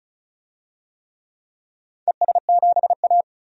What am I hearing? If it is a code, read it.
Morse code, heard as a keyed tone, about 36 wpm, 700 Hz. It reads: ES7A
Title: DXCC-100 Call Signs - 4 Characters - 35wpm